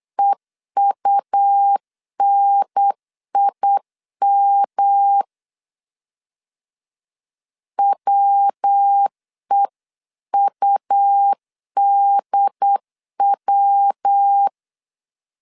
Morse code, heard as a keyed tone, about 8 wpm, 800 Hz. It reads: EUNIM WEUDW